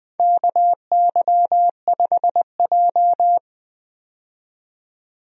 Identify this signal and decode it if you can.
Morse code, heard as a keyed tone, about 21 wpm, 700 Hz. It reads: KY5J
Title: USA Call Signs - 4 Characters - 20wpm